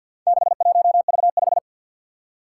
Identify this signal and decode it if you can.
Morse code, heard as a keyed tone, about 52 wpm, 700 Hz. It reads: 61V5
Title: Four Challenging Letter-Number Combinations 50wpm